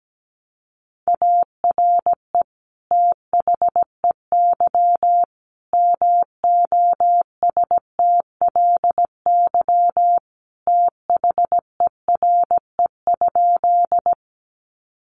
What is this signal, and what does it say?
Morse code, heard as a keyed tone, about 17 wpm, 700 Hz. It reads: ARE THEY MOSTLY THERE?
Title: Sentences from Top 1300 Words 17wpm